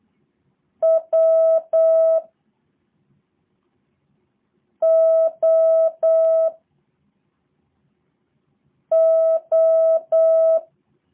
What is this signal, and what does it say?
Morse code, heard as a keyed tone, about 8 wpm, 650 Hz. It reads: W O O
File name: jh8cbh / 2023cw / review1-4